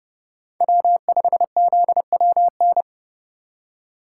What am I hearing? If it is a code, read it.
Morse code, heard as a keyed tone, about 29 wpm, 700 Hz. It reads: W5ZWN